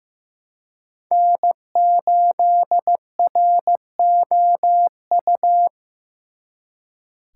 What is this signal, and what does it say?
Morse code, heard as a keyed tone, about 18 wpm, 700 Hz. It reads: N8ROU
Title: Speed Racing - USA 1x3 Call Signs 15wpm